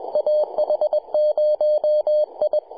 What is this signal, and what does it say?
Morse code, heard as a keyed tone, about 21 wpm, 600 Hz. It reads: AH0I
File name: 2014z AH0I 10.105MHz CW